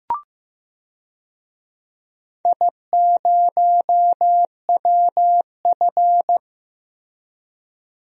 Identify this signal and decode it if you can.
Morse code, heard as a keyed tone, about 15 wpm, 700 Hz. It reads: I0WF
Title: DXCC-100 Call Signs - 4 Characters - Repeated 2x - 15wpm